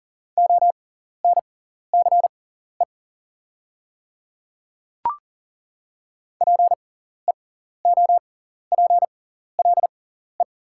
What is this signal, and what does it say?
Morse code, heard as a keyed tone, about 40 wpm, spaced as for 14 wpm, 700 Hz. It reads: ONCE PEOPLE
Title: 6x Character Spacing - Top 300 Words - 40wpm